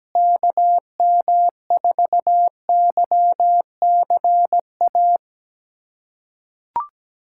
Morse code, heard as a keyed tone, about 17 wpm, 700 Hz. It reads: KM4YCA
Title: USA Callsign DE Callsign - 6 Characters - 17wpm